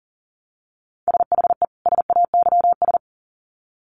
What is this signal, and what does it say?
Morse code, heard as a keyed tone, about 40 wpm, 700 Hz. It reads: SHE SAYS
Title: Sets of 2 Words 40wpm